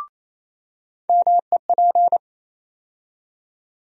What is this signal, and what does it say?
Morse code, heard as a keyed tone, about 28 wpm, 700 Hz. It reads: MEP
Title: USA Call Sign Postfixes - 3 Letters 28wpm